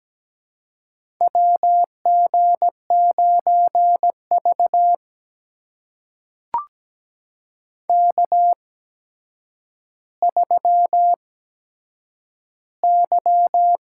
Morse code, heard as a keyed tone, about 17 wpm, spaced as for 8 wpm, 700 Hz. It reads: WG9V K3Y
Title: USA Call Signs - 4 Characters - 8x Character Spacing - 17wpm